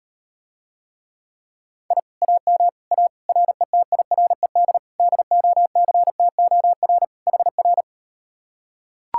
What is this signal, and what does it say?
Morse code, heard as a keyed tone, about 38 wpm, 700 Hz. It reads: I AM A RETIRED DOCTOR HR